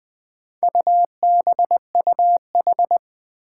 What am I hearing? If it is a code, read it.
Morse code, heard as a keyed tone, about 20 wpm, 700 Hz. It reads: UBUH
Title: Four Challenging Letter-Number Combinations 20wpm